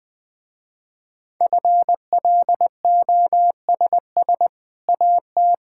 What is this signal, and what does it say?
Morse code, heard as a keyed tone, about 20 wpm, 700 Hz. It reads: FLOSS AT
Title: Easy Sentences with Hints - 20wpm